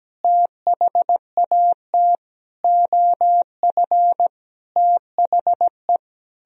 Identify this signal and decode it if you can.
Morse code, heard as a keyed tone, about 17 wpm, 700 Hz. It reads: THAT OF THE
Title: Sentences from Top 1000 Words - Review Part 2 - 17wpm